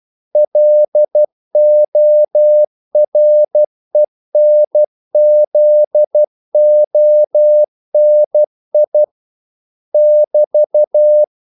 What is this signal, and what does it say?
Morse code, heard as a keyed tone, about 12 wpm, 600 Hz. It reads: LORENZONI <BT>
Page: English quotations in CW (telegraphy - morse)